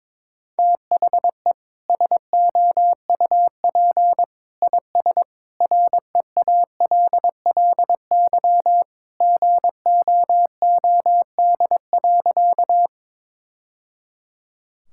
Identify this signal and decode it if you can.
Morse code, heard as a keyed tone, about 22 wpm, 700 Hz. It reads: THE SOUP IS REALLY GOOD.
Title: Sentences from Top 1000 Words - Review Part 6 - 22wpm